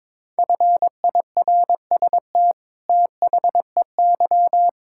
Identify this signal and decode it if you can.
Morse code, heard as a keyed tone, about 22 wpm, 700 Hz. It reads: FIRST THEY
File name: practice / 22wpm / Sets of 8 Words - 22wpm